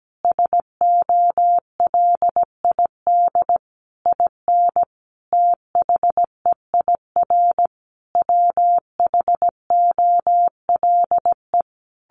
Morse code, heard as a keyed tone, about 17 wpm, 700 Hz. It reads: SOLID IN THEIR WHOLE
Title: Sentences from Top 1500 Words 17wpm